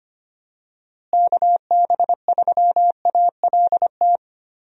Morse code, heard as a keyed tone, about 25 wpm, 700 Hz. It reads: KB3ALT
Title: USA Call Signs - 6 Characters - Repeated 3x - 25wpm